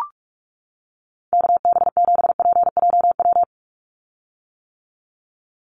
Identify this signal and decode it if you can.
Morse code, heard as a keyed tone, about 44 wpm, 700 Hz. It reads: KB7PJW
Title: USA Call Signs 45wpm